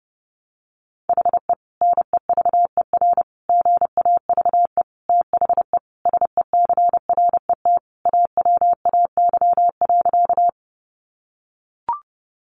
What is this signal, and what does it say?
Morse code, heard as a keyed tone, about 30 wpm, 700 Hz. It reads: HE NEVER GAVE THE SECRET AWAY.
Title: Sentences from Top 1200 Words 30wpm